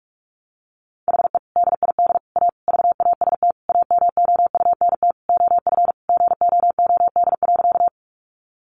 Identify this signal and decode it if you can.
Morse code, heard as a keyed tone, about 44 wpm, 700 Hz. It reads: HE DID A VAST AMOUNT OF GOOD.
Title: Sentences from Top 1300 Words 45wpm